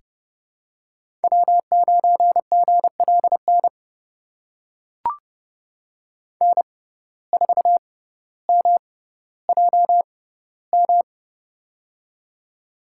Morse code, heard as a keyed tone, about 30 wpm, spaced as for 15 wpm, 700 Hz. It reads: W9GLN N4MJM
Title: USA Call Signs - 5 Characters - 6x Character Spacing - 30wpm